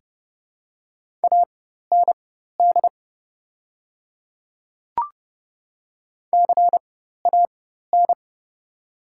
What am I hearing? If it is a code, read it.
Morse code, heard as a keyed tone, about 30 wpm, spaced as for 14 wpm, 700 Hz. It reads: AND CAN
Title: 4x Character Spacing - Top 300 Words - 30wpm